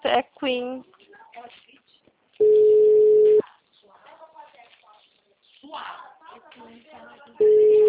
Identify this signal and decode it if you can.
Ringback tone (Europe, 425 Hz), with speech.
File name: out- -2032- - - .60344